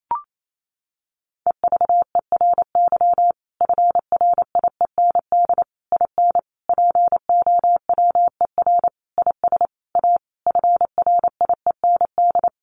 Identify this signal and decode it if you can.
Morse code, heard as a keyed tone, about 28 wpm, 700 Hz. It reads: EVERY FRIEND IN POWER IS A FRIEND
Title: Sentences from Top 1400 Words 28wpm